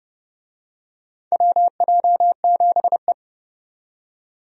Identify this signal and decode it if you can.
Morse code, heard as a keyed tone, about 31 wpm, 700 Hz. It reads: WJ7E
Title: USA Call Signs - 4 Characters - 30wpm